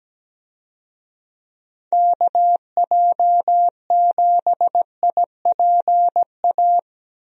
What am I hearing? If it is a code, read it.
Morse code, heard as a keyed tone, about 17 wpm, 700 Hz. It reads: KJ7IPA
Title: USA Call Signs - 6 Characters - Repeated 2x - 17wpm